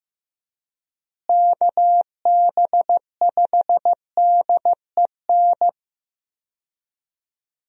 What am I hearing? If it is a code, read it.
Morse code, heard as a keyed tone, about 15 wpm, 700 Hz. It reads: KB5DEN
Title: USA Call Signs - 6 Characters - Repeated 2x - 15wpm